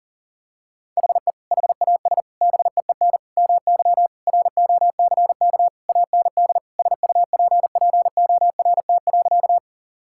Morse code, heard as a keyed tone, about 40 wpm, 700 Hz. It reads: HE HAS BEEN MY ROCK AND SUPPORT.
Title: Sentences from Top 1000 Words - Review Part 5 - 40wpm